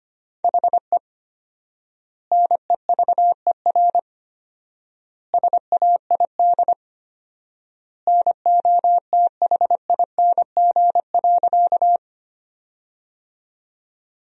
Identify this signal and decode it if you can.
Morse code, heard as a keyed tone, about 25 wpm, 700 Hz. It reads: HE NEVER SAID NOTHING.
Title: 4x Word-Spacing - Sentences from Top 500 Words - 25wpm